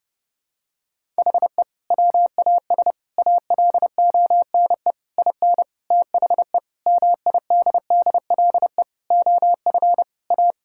Morse code, heard as a keyed tone, about 30 wpm, 700 Hz. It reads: HE WAS ALONE IN THE MIDDLE OF A
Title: Sentences from Top 1000 Words - Review Part 6 - 30wpm